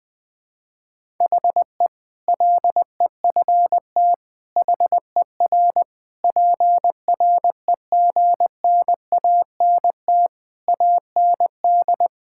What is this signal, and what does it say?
Morse code, heard as a keyed tone, about 20 wpm, 700 Hz. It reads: HE LEFT HER PREGNANT AND
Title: Sentences from Top 1000 Words - Review Part 6 - 20wpm